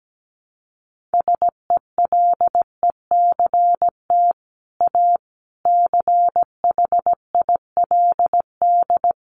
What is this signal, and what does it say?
Morse code, heard as a keyed tone, about 17 wpm, 700 Hz. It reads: SELECT A CHILD
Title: Sentences from Top 1300 Words 17wpm